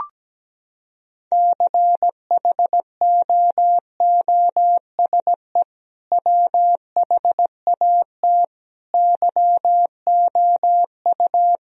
Morse code, heard as a keyed tone, about 17 wpm, 700 Hz. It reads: CHOOSE WHAT YOU
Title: Sentences from Top 1000 Words - Review Part 4 - 17wpm